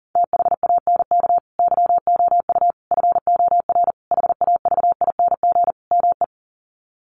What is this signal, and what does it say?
Morse code, heard as a keyed tone, about 37 wpm, 700 Hz. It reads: THANK YOU FOR HAVING ME
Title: Sets of 5 Words Encore 40wpm